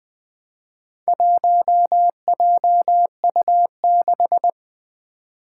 Morse code, heard as a keyed tone, about 20 wpm, 700 Hz. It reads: 1JU6
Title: Four Challenging Letter-Number Combinations 20wpm